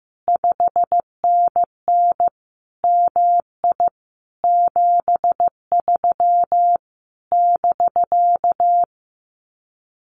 Morse code, heard as a keyed tone, about 15 wpm, 700 Hz. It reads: 5NN MI 73 <BK>
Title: Common QSO Phrases Encore 15wpm